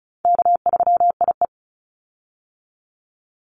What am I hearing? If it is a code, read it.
Morse code, heard as a keyed tone, about 34 wpm, 700 Hz. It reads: K3IE
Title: Easy Call Signs Repeated 3x - 35wpm